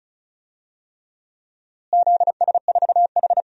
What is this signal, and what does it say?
Morse code, heard as a keyed tone, about 34 wpm, 700 Hz. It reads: ZS4H
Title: DXCC-100 Call Signs - 4 Characters - Repeated 3x - 35wpm